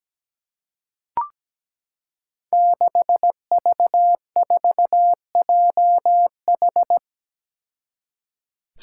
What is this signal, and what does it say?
Morse code, heard as a keyed tone, about 17 wpm, 700 Hz. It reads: 6V4JH
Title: Five Challenging Letter-Number Combinations 17wpm